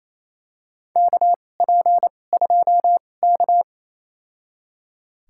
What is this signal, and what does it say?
Morse code, heard as a keyed tone, about 28 wpm, spaced as for 20 wpm, 700 Hz. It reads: KP2K
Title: USA Call Signs - 4 Characters - 2x Character Spacing - 28wpm